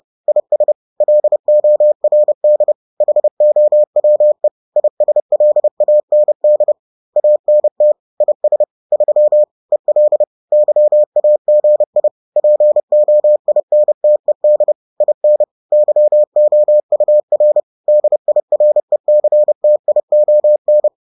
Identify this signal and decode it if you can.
Morse code, heard as a keyed tone, about 30 wpm, 600 Hz. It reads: IS LORD HOWE ISLAND ANT IS 3 EL YAGI POINTED IN YOUR DIRECTION